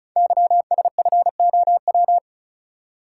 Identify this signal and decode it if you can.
Morse code, heard as a keyed tone, about 36 wpm, 700 Hz. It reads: YSFOW
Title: Code Groups - 5 Letter:Numbers - 35wpm